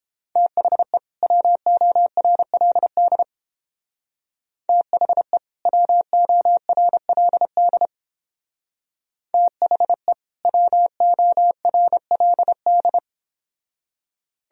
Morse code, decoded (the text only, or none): THE WORLD THE WORLD THE WORLD